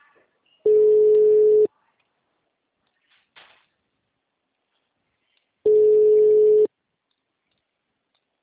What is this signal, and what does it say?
Ringback tone (Europe, 425 Hz).